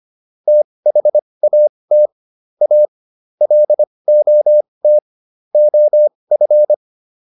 Morse code, heard as a keyed tone, about 25 wpm, spaced as for 20 wpm, 600 Hz. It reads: THAT A LOT OF